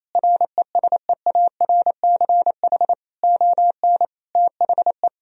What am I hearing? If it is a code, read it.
Morse code, heard as a keyed tone, about 28 wpm, 700 Hz. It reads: RESEARCH ON THE
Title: Sentences from Top 1000 Words - Review Part 4 - 28wpm